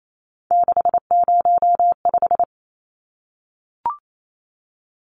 Morse code, heard as a keyed tone, about 28 wpm, 700 Hz. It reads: 605
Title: Numbers - 3-Digit 28wpm